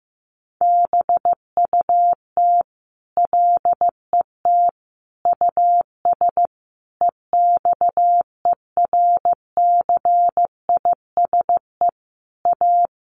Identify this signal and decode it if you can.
Morse code, heard as a keyed tone, about 15 wpm, 700 Hz. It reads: BUT LET US EXERCISE A
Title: Sentences from Top 1300 Words 15wpm